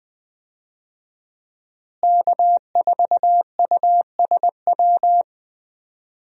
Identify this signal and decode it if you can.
Morse code, heard as a keyed tone, about 20 wpm, 700 Hz. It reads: K4USW
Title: USA Call Signs - 5 Characters - Repeated 3x - 20wpm